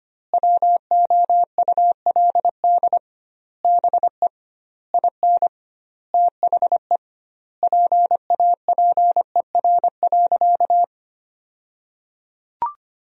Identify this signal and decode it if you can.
Morse code, heard as a keyed tone, about 25 wpm, 700 Hz. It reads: WOULD BE IN THE PAPER.
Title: 2x Word-Spacing - Sentences from Top 300 Words - 25wpm